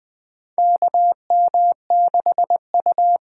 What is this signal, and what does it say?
Morse code, heard as a keyed tone, about 20 wpm, 700 Hz. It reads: KM6U
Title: USA Call Signs - 6 Characters - 8x Character Spacing - 20wpm